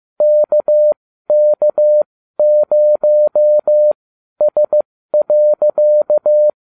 Morse code, heard as a keyed tone, about 15 wpm, spaced as for 12 wpm, 600 Hz. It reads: KK0S.